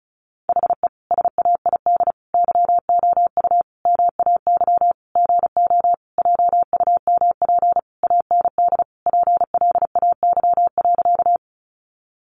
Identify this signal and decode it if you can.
Morse code, heard as a keyed tone, about 34 wpm, 700 Hz. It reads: HE SAID YOU MAY GO JUMP AND PLAY.
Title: Sentences from Top 1300 Words 35wpm